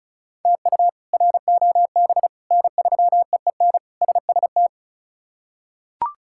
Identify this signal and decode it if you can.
Morse code, heard as a keyed tone, about 34 wpm, 700 Hz. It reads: TU ROB N3EEN SST
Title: K1USN SST QSO Practice - 35wpm